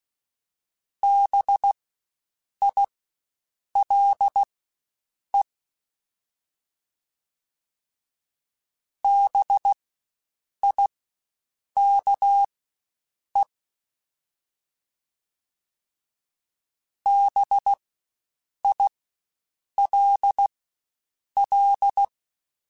Morse code, heard as a keyed tone, about 16 wpm, spaced as for 7 wpm, 800 Hz. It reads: BILE BIKE BILL
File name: Spaced Alphas / Spaced Alphas mp3s / Bs sp @15